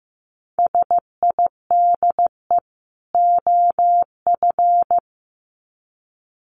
Morse code, heard as a keyed tone, about 15 wpm, 700 Hz. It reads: SIDE OF
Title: Sets of 2 Words 15wpm